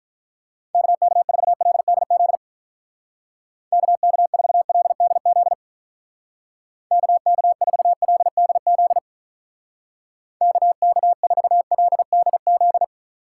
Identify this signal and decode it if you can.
Morse code, heard as a keyed tone, about 43 wpm, 700 Hz. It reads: KK4LDZ KK4LDZ KK4LDZ KK4LDZ
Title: Speed Racing - USA 2x3 Call Signs 35wpm